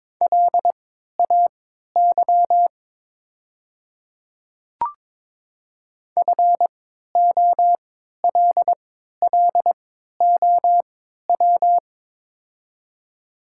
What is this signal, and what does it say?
Morse code, heard as a keyed tone, about 22 wpm, spaced as for 13 wpm, 700 Hz. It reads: LAY FOLLOW
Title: 3x Character Spacing - Top 500 Words - 22wpm